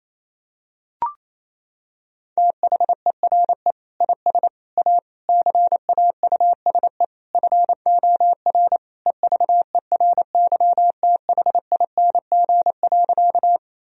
Morse code, heard as a keyed tone, about 28 wpm, 700 Hz. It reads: THERE IS A CAUSE FOR EVERYTHING.
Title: Sentences from Top 1000 Words - Review Part 1 - 28wpm